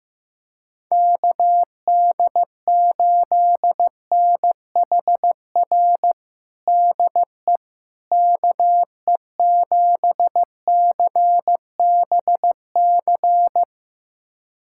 Morse code, heard as a keyed tone, about 15 wpm, 700 Hz. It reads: KD8NHR DE KE7CBC
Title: USA Callsign DE Callsign 15wpm